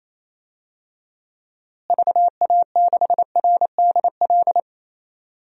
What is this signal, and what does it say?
Morse code, heard as a keyed tone, about 28 wpm, 700 Hz. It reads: VA6RDL
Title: DXCC-100 Call Signs - 6 Characters - Repeated 3x - 28wpm